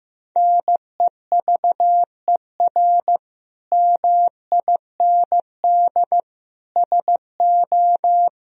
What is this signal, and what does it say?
Morse code, heard as a keyed tone, about 15 wpm, 700 Hz. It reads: NEVER MIND SO
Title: Sentences from Top 1000 Words - Review Part 1 - 15wpm